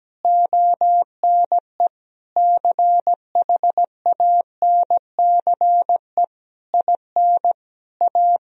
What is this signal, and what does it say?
Morse code, heard as a keyed tone, about 17 wpm, 700 Hz. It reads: ONE CHANCE IN A
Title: Sentences from Top 1000 Words - Review Part 3 - 17wpm